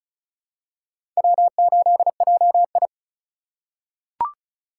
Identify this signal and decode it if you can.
Morse code, heard as a keyed tone, about 36 wpm, 700 Hz. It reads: W8JI
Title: USA Call Signs - 4 Characters - 4x Character Spacing - 35wpm